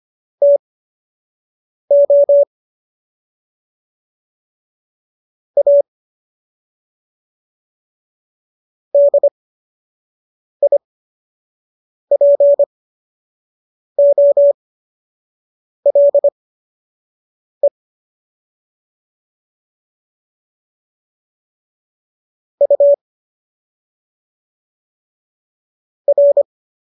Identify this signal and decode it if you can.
Morse code, heard as a keyed tone, about 25 wpm, spaced as for 6 wpm, 570 Hz. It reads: TO A DIPOLE U R